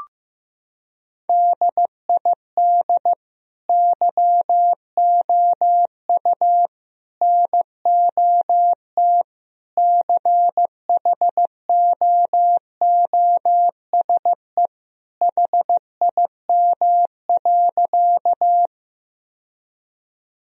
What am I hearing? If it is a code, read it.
Morse code, heard as a keyed tone, about 15 wpm, 700 Hz. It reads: DID YOU NOT CHOOSE HIM.
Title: Sentences from Top 1000 Words - Review Part 4 - 15wpm